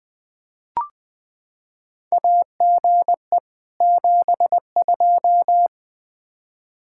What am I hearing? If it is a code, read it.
Morse code, heard as a keyed tone, about 20 wpm, 700 Hz. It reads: AGE 72
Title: K1USN SST QSO Practice 20wpm